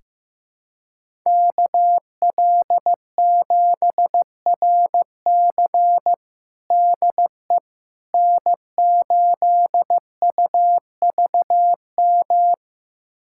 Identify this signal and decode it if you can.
Morse code, heard as a keyed tone, about 15 wpm, 700 Hz. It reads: KL7RC DE N8UVM
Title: USA Callsign DE Callsign - 5 Characters - 15wpm